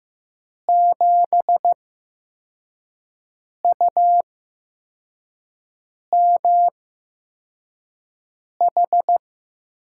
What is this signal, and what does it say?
Morse code, heard as a keyed tone, about 15 wpm, spaced as for 4 wpm, 700 Hz. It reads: 7UMH
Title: USA Call Signs - 6 Characters - 8x Character Spacing - 15wpm